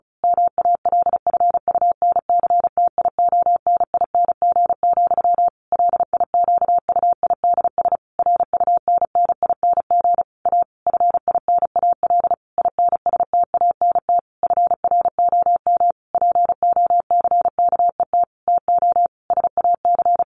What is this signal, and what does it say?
Morse code, heard as a keyed tone, about 34 wpm, 700 Hz. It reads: MALFUNCTIONING, LIQUIDS RUNNING A FINAL INSTANT FROM POCKET TO SAC
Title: A Sound Of Thunder 35wpm